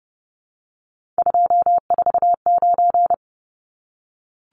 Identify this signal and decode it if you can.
Morse code, heard as a keyed tone, about 30 wpm, 700 Hz. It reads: 249
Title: Numbers - 3-Digit 30wpm